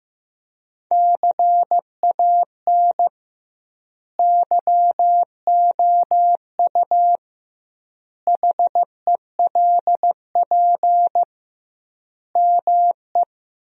Morse code, heard as a keyed tone, about 15 wpm, 700 Hz. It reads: CAN YOU HELP ME
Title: 2x Word-Spacing - Sentences from Top 200 Words - 15wpm